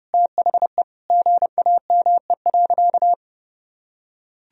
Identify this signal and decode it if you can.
Morse code, heard as a keyed tone, about 30 wpm, 700 Hz. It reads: THE GAME.
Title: Sentences from Top 1000 Words - Review Part 4 - 30wpm